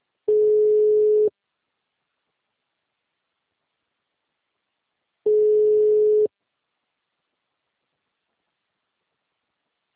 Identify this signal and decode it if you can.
Ringback tone (Europe, 425 Hz).